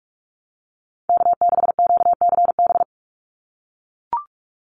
Morse code, heard as a keyed tone, about 44 wpm, 700 Hz. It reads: K6QCB
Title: NA SOTA Chaser Callsigns 45wpm